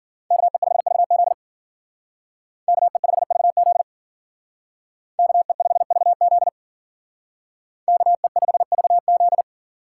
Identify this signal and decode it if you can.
Morse code, heard as a keyed tone, about 52 wpm, 700 Hz. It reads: KE5VZ KE5VZ KE5VZ KE5VZ